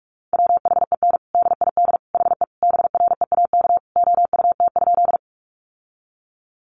Morse code, heard as a keyed tone, about 44 wpm, 700 Hz. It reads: WHEN DID HE BREAK OUT?
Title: Sentences from Top 900 Words 45wpm